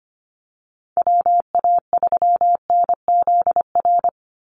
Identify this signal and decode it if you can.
Morse code, heard as a keyed tone, about 25 wpm, 700 Hz. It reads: WA3NZR
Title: Call Signs - 25wpm